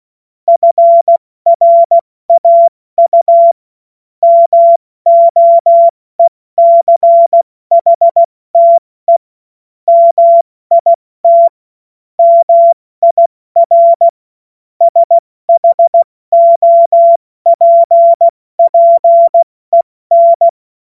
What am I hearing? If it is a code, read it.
Morse code, heard as a keyed tone, about 16 wpm, 680 Hz. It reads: FRAU MOECHTE MIT MIR SHOPPEN
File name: Morsekurs / Witze / Witz 04. April 70 F80